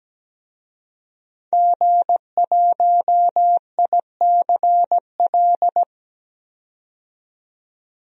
Morse code, heard as a keyed tone, about 17 wpm, 700 Hz. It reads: G1ICL
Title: DXCC-100 Call Signs - 5 Characters - 17wpm